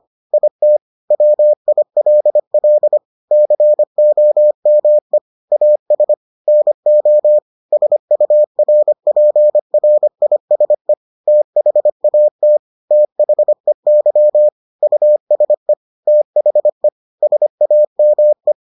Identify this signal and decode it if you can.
Morse code, heard as a keyed tone, about 25 wpm, 600 Hz. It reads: IT WILL COME AS NO SURPRISE THAT THEY USE THE SAME